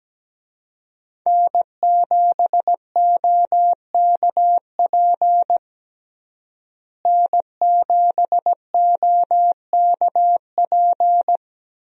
Morse code, heard as a keyed tone, about 17 wpm, 700 Hz. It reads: N7OKP N7OKP
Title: USA Call Signs - 5 Characters - Repeated 2x - 17wpm